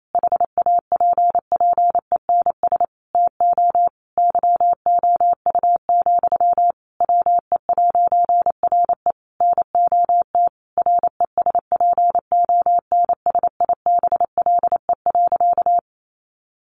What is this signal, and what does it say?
Morse code, heard as a keyed tone, about 28 wpm, 700 Hz. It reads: HAPPENS TO YOU, WE'RE NOT RESPONSIBLE.
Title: A Sound Of Thunder 28wpm